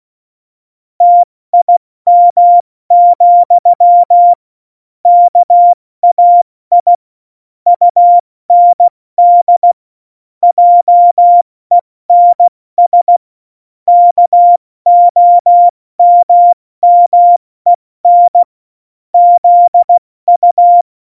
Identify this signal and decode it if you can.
Morse code, heard as a keyed tone, about 16 wpm, 700 Hz. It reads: TIM, KAI UND JENS KOMMEN ZU
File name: Morsekurs / Witze / Witz 30. April 2019, 70 F80